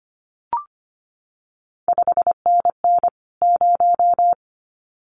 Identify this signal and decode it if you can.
Morse code, heard as a keyed tone, about 25 wpm, 700 Hz. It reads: 5NN 0
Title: CQ WW DX Contest Exchange Encore - 25wpm